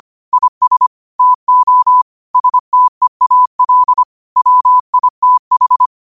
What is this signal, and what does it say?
Morse code, heard as a keyed tone, about 25 wpm, 1000 Hz. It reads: IS TO STEAL WITH
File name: qotdfiles / 25wpm